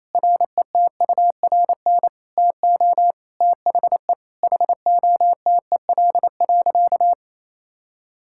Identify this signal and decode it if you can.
Morse code, heard as a keyed tone, about 28 wpm, 700 Hz. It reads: RETURN TO THE HOTEL.
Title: Sentences from Top 1600 Words 28wpm